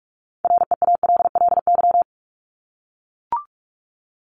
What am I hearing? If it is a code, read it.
Morse code, heard as a keyed tone, about 44 wpm, 700 Hz. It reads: REALLY
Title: LY Suffix - Top 100 Words with Base Words 45wpm